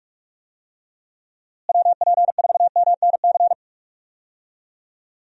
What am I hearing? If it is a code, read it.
Morse code, heard as a keyed tone, about 41 wpm, 700 Hz. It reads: WP4MNC
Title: USA Call Signs - 6 Characters - Repeated 2x - 45wpm